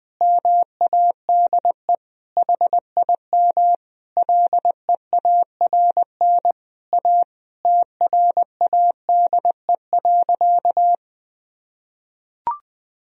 Morse code, heard as a keyed tone, about 20 wpm, 700 Hz. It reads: MADE HIM LEARN A TRADE.
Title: Sentences from Top 1000 Words - Review Part 4 - 20wpm